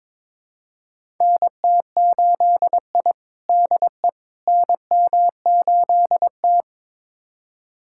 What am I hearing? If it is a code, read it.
Morse code, heard as a keyed tone, about 22 wpm, 700 Hz. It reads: NT8I DE NM8T
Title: USA Callsign DE Callsign - 4 Characters - 22wpm